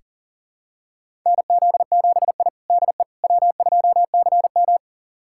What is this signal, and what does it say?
Morse code, heard as a keyed tone, about 40 wpm, 700 Hz. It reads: NZ7I DE W2CM
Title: USA Callsign DE Callsign - 4 Characters - 40wpm